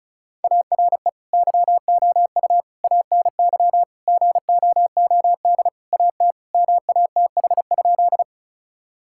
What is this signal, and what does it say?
Morse code, heard as a keyed tone, about 34 wpm, 700 Hz. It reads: ARE YOU ANY GOOD AT MATH?
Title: Sentences from Top 1000 Words - Review Part 6 - 35wpm